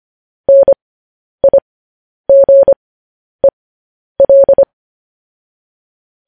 Morse code, heard as a keyed tone, about 25 wpm, spaced as for 10 wpm, 550 Hz. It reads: NIGEL
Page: News Headlines in Morse Code at 10 WPM